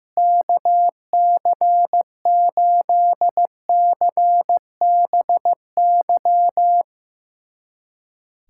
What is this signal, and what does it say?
Morse code, heard as a keyed tone, about 16 wpm, 700 Hz. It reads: KC8CBY
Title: USA Callsign DE Callsign 15wpm